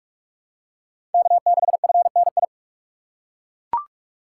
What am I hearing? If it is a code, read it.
Morse code, heard as a keyed tone, about 42 wpm, 700 Hz. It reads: K6FNI
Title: USA Call Signs - 3x Character Spacing - 45wpm